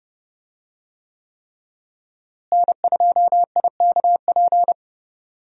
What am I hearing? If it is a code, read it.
Morse code, heard as a keyed tone, about 31 wpm, 700 Hz. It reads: N2IKP
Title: USA Call Signs - 5 Characters - Repeated 3x - 30wpm